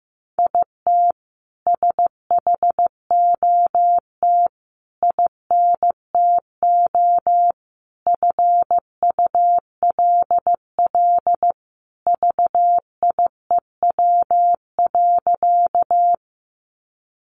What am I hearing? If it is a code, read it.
Morse code, heard as a keyed tone, about 14 wpm, 700 Hz. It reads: IT SHOT INTO FULL VIEW.
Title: Sentences from Top 1100 Words 15wpm